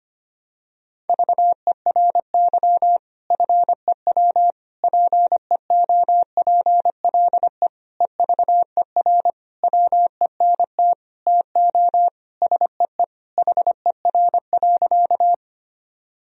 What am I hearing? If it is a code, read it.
Morse code, heard as a keyed tone, about 25 wpm, 700 Hz. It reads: VERY FEW PEOPLE EVER WENT TO SEE HER.
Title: Sentences from Top 1000 Words - Review Part 1 - 25wpm